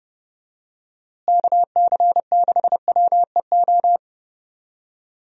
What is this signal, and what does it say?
Morse code, heard as a keyed tone, about 30 wpm, 700 Hz. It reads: KC6WEO
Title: USA Call Signs - 6 Characters - Repeated 2x - 30wpm